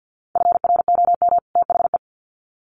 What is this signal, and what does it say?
Morse code, heard as a keyed tone, about 50 wpm, 700 Hz. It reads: FROM THE
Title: Sets of 2 Words 50wpm